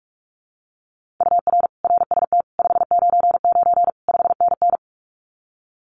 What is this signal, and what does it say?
Morse code, heard as a keyed tone, about 46 wpm, 700 Hz. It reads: UR RST 599 5NN
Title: Common QSO Phrases Encore 45wpm